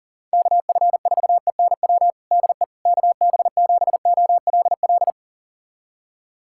Morse code, heard as a keyed tone, about 40 wpm, 700 Hz. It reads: KF4ENW DE KB7OLL
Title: USA Callsign DE Callsign - 6 Characters - 40wpm